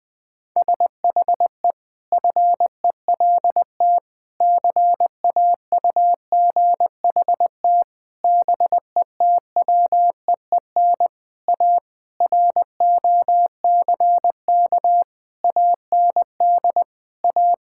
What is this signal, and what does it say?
Morse code, heard as a keyed tone, about 20 wpm, 700 Hz. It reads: SHE FELT CAUGHT BETWEEN A ROCK AND A
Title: Sentences with 8-Word Phrases - 20wpm